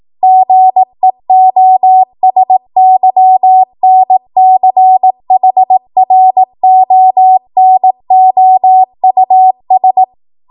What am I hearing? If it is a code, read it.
Morse code, heard as a keyed tone, about 18 wpm, 750 Hz. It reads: GEOSYNCHRONOUS
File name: Morse / 18WPM